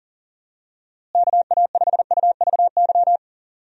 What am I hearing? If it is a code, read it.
Morse code, heard as a keyed tone, about 40 wpm, 700 Hz. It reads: KA5UVY
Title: USA Call Signs - 6 Characters - Repeated 2x - 40wpm